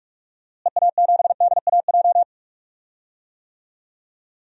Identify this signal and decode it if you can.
Morse code, heard as a keyed tone, about 43 wpm, 700 Hz. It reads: EA7DAJ
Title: DXCC-100 Call Signs - 6 Characters - Repeated 2x - 45wpm